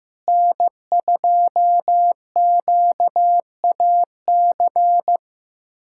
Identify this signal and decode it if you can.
Morse code, heard as a keyed tone, about 15 wpm, 700 Hz. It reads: N2QAC
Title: USA Callsign DE Callsign - 5 Characters - 15wpm